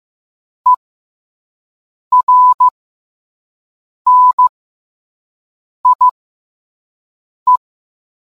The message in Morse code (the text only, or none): ERNIE